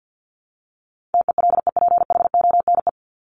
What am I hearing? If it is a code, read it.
Morse code, heard as a keyed tone, about 50 wpm, 700 Hz. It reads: TELEPHONE
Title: Top 1501-1750 Words 50wpm